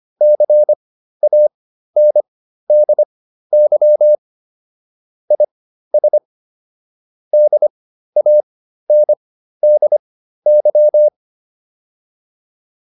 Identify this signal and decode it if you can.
Morse code, heard as a keyed tone, about 25 wpm, spaced as for 13 wpm, 600 Hz. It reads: CANDY IS DANDY